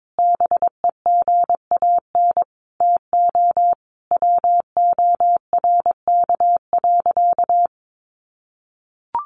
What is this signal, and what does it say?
Morse code, heard as a keyed tone, about 22 wpm, 700 Hz. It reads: BEGAN TO WORK.
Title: Sentences from Top 1100 Words 22wpm